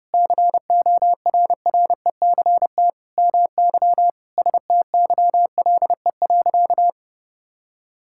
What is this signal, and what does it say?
Morse code, heard as a keyed tone, about 30 wpm, 700 Hz. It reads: CORRECT MY STYLE.
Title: Sentences from Top 1000 Words - Review Part 5 - 30wpm